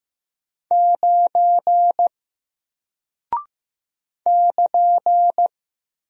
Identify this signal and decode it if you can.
Morse code, heard as a keyed tone, about 15 wpm, 700 Hz. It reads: 9 <KN>
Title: Single Character (Letter:Number:Punctuation:Prosign) Alt 15wpm